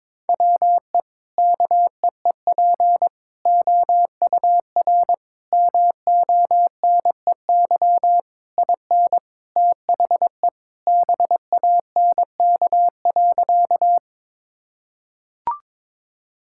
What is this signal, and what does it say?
Morse code, heard as a keyed tone, about 22 wpm, 700 Hz. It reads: WE KEEP OUR MONEY IN THE BANK.
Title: Sentences from Top 1000 Words - Review Part 6 - 22wpm